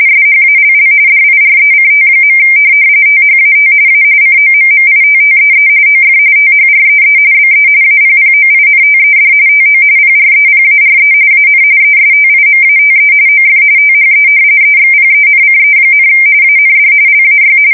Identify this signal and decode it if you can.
RTTY, about 45.45 baud. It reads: SHALL DIRECT. THAT A COMMITTEE OF THREE BE APPOINTED TO PREPARE AN ESTIMATE OF THE EXPENCE, AND LAY
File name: fsk170 / usnavy1